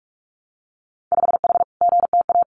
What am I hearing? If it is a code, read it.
Morse code, heard as a keyed tone, about 46 wpm, 700 Hz. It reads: 5H GTA
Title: Winter Field Day Exchanges Encore 45wpm